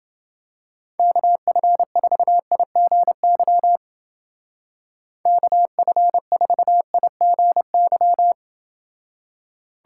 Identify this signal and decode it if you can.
Morse code, heard as a keyed tone, about 29 wpm, 700 Hz. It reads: KF4IGY KF4IGY